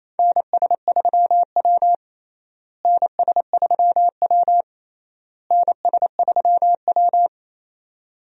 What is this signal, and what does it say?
Morse code, heard as a keyed tone, about 28 wpm, 700 Hz. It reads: NS3W NS3W NS3W
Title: USA Call Signs - 4 Characters - Repeated 3x - 28wpm